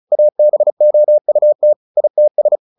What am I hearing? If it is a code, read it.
Morse code, heard as a keyed tone, about 34 wpm, 600 Hz. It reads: ABOUT ITS
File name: the old man and the sea 2 35wpm